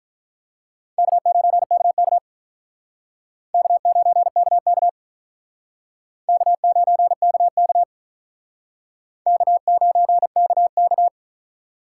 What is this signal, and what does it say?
Morse code, heard as a keyed tone, about 45 wpm, 700 Hz. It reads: K9KK K9KK K9KK K9KK